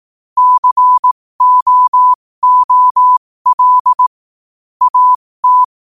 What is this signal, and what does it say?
Morse code, heard as a keyed tone, about 18 wpm, spaced as for 15 wpm, 1000 Hz. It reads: COOL AT
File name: qotdfiles / 15wpm